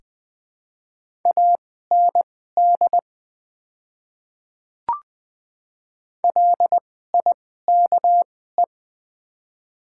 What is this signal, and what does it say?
Morse code, heard as a keyed tone, about 20 wpm, spaced as for 14 wpm, 700 Hz. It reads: AND LIKE
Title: 2x Character Spacing - Top 100 Words - 20wpm